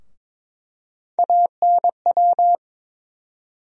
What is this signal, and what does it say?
Morse code, heard as a keyed tone, about 22 wpm, 700 Hz. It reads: ANW
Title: USA Call Sign Postfixes - 3 Letters 22wpm